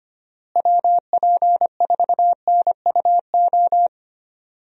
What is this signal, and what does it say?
Morse code, heard as a keyed tone, about 25 wpm, 700 Hz. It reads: WP4NUO
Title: USA Call Signs - 6 Characters - Repeated 3x - 25wpm